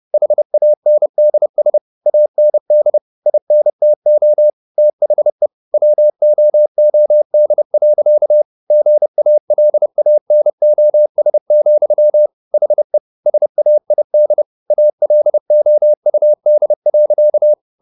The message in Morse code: HANDS AND INTO THE WOOD. GALANOS, HE SAID ALOUD.

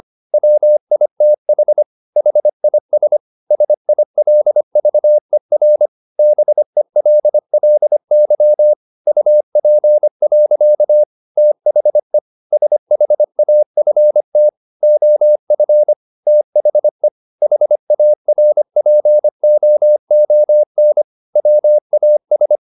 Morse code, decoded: WITH HIS SILVER BELLY UP. THE SHAFT OF THE HARPOON WAS

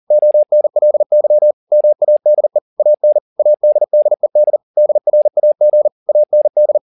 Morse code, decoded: ONLY MADE AN ADDED DRAG AND

